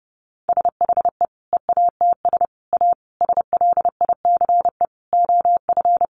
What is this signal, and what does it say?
Morse code, heard as a keyed tone, about 30 wpm, 700 Hz. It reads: SHE EATS A SLICE OF
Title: Easy Phrases 30wpm